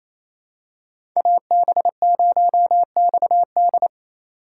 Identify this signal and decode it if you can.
Morse code, heard as a keyed tone, about 28 wpm, 700 Hz. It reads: AB0XD
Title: USA Call Signs - 5 Characters - 2x Character Spacing - 28wpm